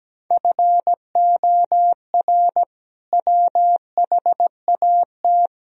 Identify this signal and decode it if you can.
Morse code, heard as a keyed tone, about 17 wpm, 700 Hz. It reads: FOR WHAT
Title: Sentences from Top 1000 Words - Review Part 4 - 17wpm